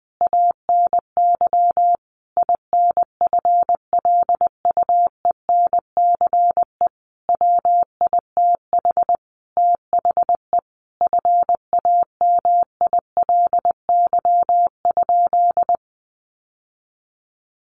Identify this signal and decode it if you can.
Morse code, heard as a keyed tone, about 20 wpm, 700 Hz. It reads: ANY INFLUENCE WITH THE FAMILY?
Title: Sentences from Top 1100 Words 20wpm